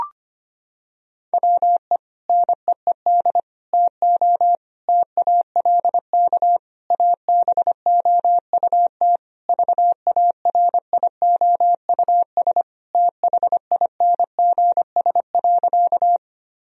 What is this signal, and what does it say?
Morse code, heard as a keyed tone, about 25 wpm, 700 Hz. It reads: WE NEED TO TALK ABOUT VARIOUS THINGS.
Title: Sentences from Top 1000 Words - Review Part 4 - 25wpm